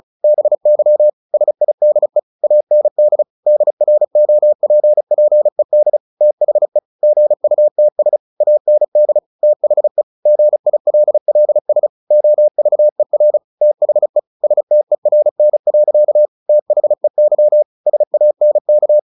Morse code, heard as a keyed tone, about 34 wpm, 600 Hz. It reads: BY SIDE AND DROPPED THE GUTS AND THE GILLS OVER THE STERN. THEY SANK
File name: the old man and the sea 2 35wpm